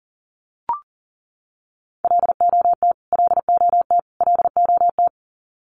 Morse code, heard as a keyed tone, about 40 wpm, 700 Hz. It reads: LOT LOT LOT
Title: Top 500 Words Repeated 40wpm